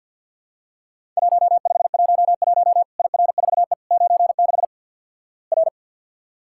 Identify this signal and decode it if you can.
Morse code, heard as a keyed tone, about 48 wpm, 700 Hz. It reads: 1511 IR4E 96 R
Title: Worked All Europe DX Contest - QTC Exchange - 50wpm